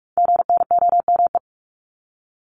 Morse code, heard as a keyed tone, about 46 wpm, 700 Hz. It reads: GNOME
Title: Multi-letter Phonemes 45wpm